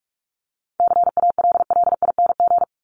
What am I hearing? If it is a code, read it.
Morse code, heard as a keyed tone, about 44 wpm, 700 Hz. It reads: CALLING
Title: ING Suffix - Top 100 Words with Base Words 45wpm